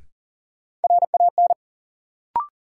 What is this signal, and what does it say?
Morse code, heard as a keyed tone, about 38 wpm, 700 Hz. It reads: RAN
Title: Sentences from Top 500 Words - New Word Drill - 40wpm